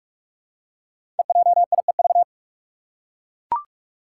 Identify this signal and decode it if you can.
Morse code, heard as a keyed tone, about 44 wpm, 700 Hz. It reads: EJIEV
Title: Five Challenging Letter-Number Combinations 45wpm